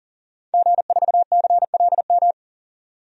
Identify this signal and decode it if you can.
Morse code, heard as a keyed tone, about 40 wpm, 700 Hz. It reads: G4CLM
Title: DXCC-100 Call Signs - 5 Characters - 40wpm